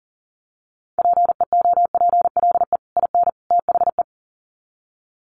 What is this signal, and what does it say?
Morse code, heard as a keyed tone, about 40 wpm, 700 Hz. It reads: PEOPLE IN THE
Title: Sets of 3 Words 40wpm